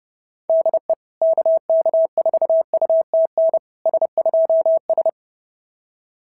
Morse code, heard as a keyed tone, about 30 wpm, 690 Hz. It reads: DE KK4UTN S2S
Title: SOTA QSO Practice 30wpm